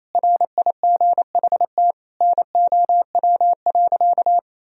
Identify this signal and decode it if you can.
Morse code, heard as a keyed tone, about 28 wpm, 700 Hz. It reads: RIGHT NOW.
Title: Sentences from Top 1000 Words - Review Part 4 - 28wpm